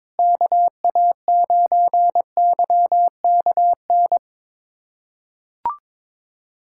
Morse code, heard as a keyed tone, about 22 wpm, 700 Hz. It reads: KA9YKN
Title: USA Callsign DE Callsign - 6 Characters - 22wpm